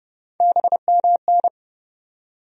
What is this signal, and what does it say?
Morse code, heard as a keyed tone, about 30 wpm, 700 Hz. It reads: BMN
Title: USA Call Sign Postfixes - 3 Letters 30wpm